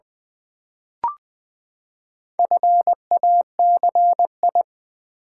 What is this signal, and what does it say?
Morse code, heard as a keyed tone, about 20 wpm, 700 Hz. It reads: FACI
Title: Build A Word - 4 Letters At A Time - 8-Letter Words - 20wpm